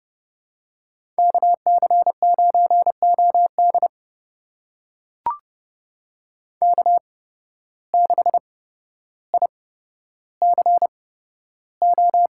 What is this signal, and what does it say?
Morse code, heard as a keyed tone, about 30 wpm, spaced as for 13 wpm, 700 Hz. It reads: KC9OD K6ICO
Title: USA Call Signs - 5 Characters - 8x Character Spacing - 30wpm